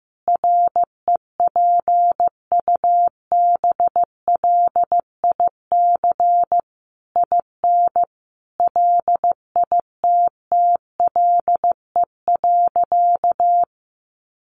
Morse code, heard as a keyed tone, about 15 wpm, 700 Hz. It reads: REPUBLIC IN LITTLE.
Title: Sentences from Top 1500 Words 15wpm